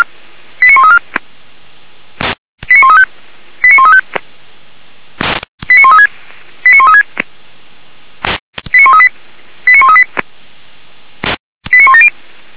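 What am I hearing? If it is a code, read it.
Five-tone selective call (ZVEI), heard as quick runs of five tones, tones of about 70 ms. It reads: 89134 89135 89135 89136 89136 89138 89138 89170
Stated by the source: ZVEI five-tone selective call received on SDR.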